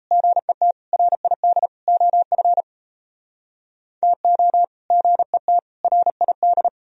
Morse code, heard as a keyed tone, about 37 wpm, 700 Hz. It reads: GET RID OF TO GET RID